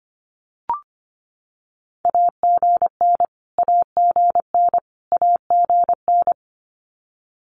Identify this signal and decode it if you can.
Morse code, heard as a keyed tone, about 25 wpm, 700 Hz. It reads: AGN AGN AGN
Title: Abbreviations Repeated 3x - 25wpm